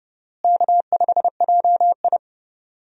Morse code, heard as a keyed tone, about 30 wpm, 700 Hz. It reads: K5JI
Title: USA Call Signs - 4 Characters - Repeated 3x - 30wpm